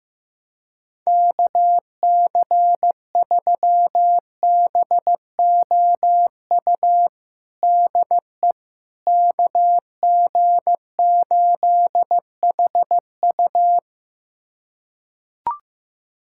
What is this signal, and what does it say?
Morse code, heard as a keyed tone, about 15 wpm, 700 Hz. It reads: KC3BOU DE KG8HU
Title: USA Callsign DE Callsign 15wpm